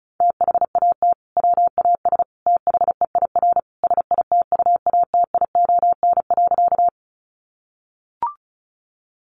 Morse code, heard as a keyed tone, about 34 wpm, 700 Hz. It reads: THAT WAS THEIR SITUATION.
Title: Sentences from Top 1200 Words 35wpm